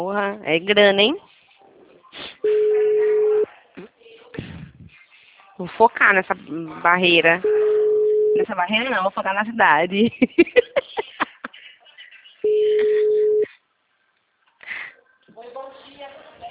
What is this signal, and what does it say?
Ringback tone (Europe, 425 Hz), with speech.
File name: out- -2020- - - .38539